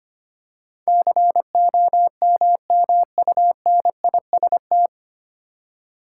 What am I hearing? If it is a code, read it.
Morse code, heard as a keyed tone, about 25 wpm, 700 Hz. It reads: COMMUNIST
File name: Nine Letter Words - 25wpm